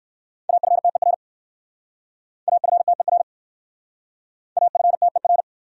Morse code, heard as a keyed tone, about 57 wpm, 700 Hz. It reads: AFTER AFTER AFTER